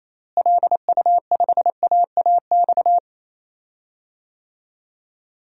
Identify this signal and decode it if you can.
Morse code, heard as a keyed tone, about 27 wpm, 700 Hz. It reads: LU5AAX
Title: DXCC-100 Call Signs - 6 Characters - Repeated 3x - 28wpm